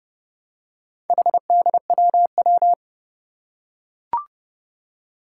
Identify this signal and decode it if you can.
Morse code, heard as a keyed tone, about 28 wpm, 700 Hz. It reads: HDWW
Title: Four Challenging Letter-Number Combinations 30wpm